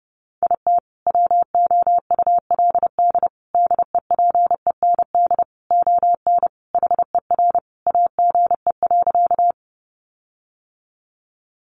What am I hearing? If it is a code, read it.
Morse code, heard as a keyed tone, about 30 wpm, 700 Hz. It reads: IT WOULD DEPEND ON HER AGE.
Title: Sentences from Top 1300 Words 30wpm